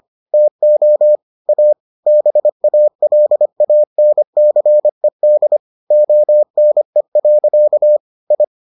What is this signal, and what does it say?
Morse code, heard as a keyed tone, about 25 wpm, 600 Hz. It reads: TO A BALANCED ONE. I